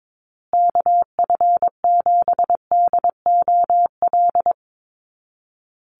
Morse code, heard as a keyed tone, about 22 wpm, 700 Hz. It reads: KF7DOL
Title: USA Call Signs Repeated 2x 22wpm